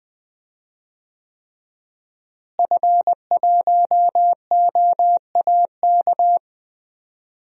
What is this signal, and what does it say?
Morse code, heard as a keyed tone, about 20 wpm, 700 Hz. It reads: F1OAK
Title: DXCC-100 Call Signs - 5 Characters - Repeated 3x - 20wpm